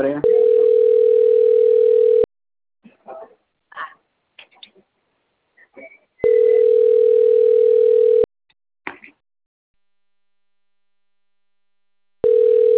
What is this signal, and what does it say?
Ringback tone (North America), 440 + 480 Hz, with speech.
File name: rg-1600-unknown- - - .27189